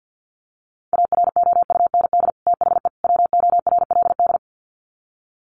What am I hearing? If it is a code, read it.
Morse code, heard as a keyed tone, about 48 wpm, 700 Hz. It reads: AROUND THE WORLD
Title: Sets of 3 Words 50wpm